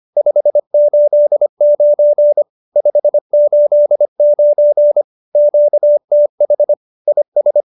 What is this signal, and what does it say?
Morse code, heard as a keyed tone, about 25 wpm, 590 Hz. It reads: 589 589 QTH IS
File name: QSO303 25